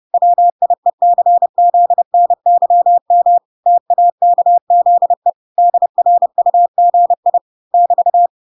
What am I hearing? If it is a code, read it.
Morse code, heard as a keyed tone, about 30 wpm, 700 Hz. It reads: WIECZNYM TAKZE DRUGI <BT>